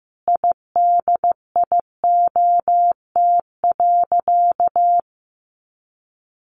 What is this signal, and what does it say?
Morse code, heard as a keyed tone, about 15 wpm, 700 Hz. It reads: IDIOT.
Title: A Sound Of Thunder 15wpm